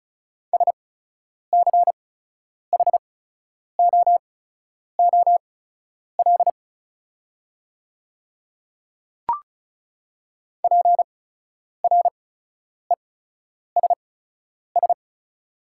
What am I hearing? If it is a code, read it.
Morse code, heard as a keyed tone, about 36 wpm, spaced as for 10 wpm, 700 Hz. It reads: SCHOOL PRESS
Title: 8x Character Spacing - Top 300 Words - 35wpm